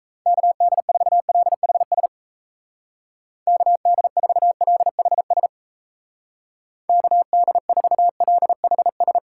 Morse code, heard as a keyed tone, about 36 wpm, 700 Hz. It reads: KD4LHS KD4LHS KD4LHS